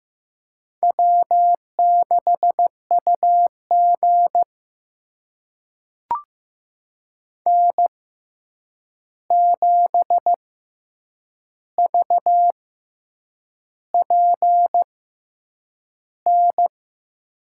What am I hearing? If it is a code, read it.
Morse code, heard as a keyed tone, about 15 wpm, spaced as for 7 wpm, 700 Hz. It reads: W6UG N7VPN
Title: USA Call Signs - 6x Character Spacing - 15wpm